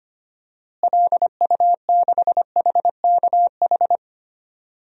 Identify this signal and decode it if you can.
Morse code, heard as a keyed tone, about 25 wpm, 700 Hz. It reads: LU6HKH
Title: DXCC-100 Call Signs - 6 Characters - Repeated 3x - 25wpm